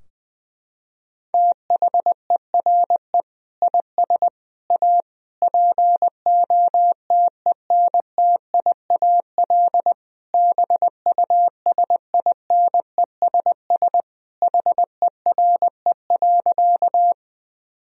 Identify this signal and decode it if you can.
Morse code, heard as a keyed tone, about 20 wpm, 700 Hz. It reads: THERE IS A POTENTIAL BUSINESS HERE.
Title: Sentences from Top 1000 Words - Review Part 4 - 20wpm